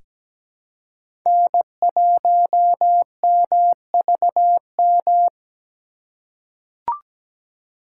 Morse code, heard as a keyed tone, about 17 wpm, 700 Hz. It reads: N1MVM
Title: USA Call Signs - 3x Character Spacing - 17wpm